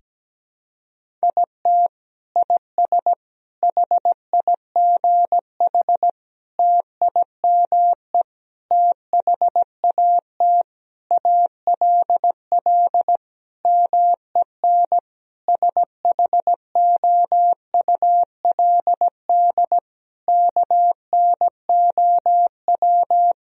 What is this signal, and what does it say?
Morse code, heard as a keyed tone, about 17 wpm, 700 Hz. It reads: IT IS HIGH TIME THAT ALL MEN SHOULD KNOW
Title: Sentences from Top 1000 Words - Review Part 1 - 17wpm